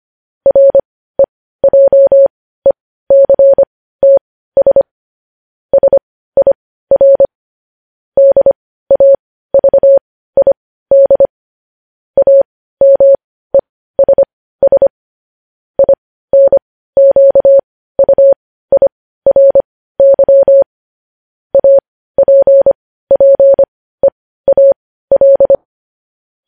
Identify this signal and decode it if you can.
Morse code, heard as a keyed tone, about 25 wpm, spaced as for 15 wpm, 550 Hz. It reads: REJECTS SIR DAVID AMESS INQUIRY APPEAL
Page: news headlines in morse code at 15 wpm